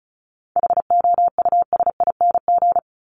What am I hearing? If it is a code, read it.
Morse code, heard as a keyed tone, about 34 wpm, 700 Hz. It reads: HOUSING
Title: Top 1001-1250 Words 35wpm